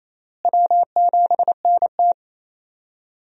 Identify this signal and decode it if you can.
Morse code, heard as a keyed tone, about 28 wpm, 700 Hz. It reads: W7NT
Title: USA Call Signs - 4 Characters - Repeated 3x - 28wpm